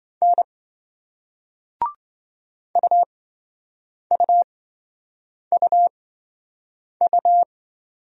Morse code, heard as a keyed tone, about 28 wpm, spaced as for 7 wpm, 700 Hz. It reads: N UUUU